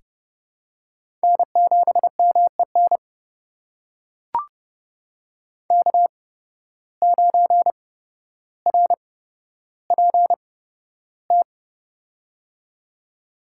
Morse code, heard as a keyed tone, about 30 wpm, spaced as for 13 wpm, 700 Hz. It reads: N7MEN K9RPT
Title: USA Call Signs - 5 Characters - 8x Character Spacing - 30wpm